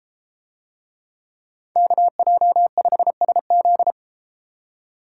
Morse code, heard as a keyed tone, about 31 wpm, 700 Hz. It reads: KJ5SZ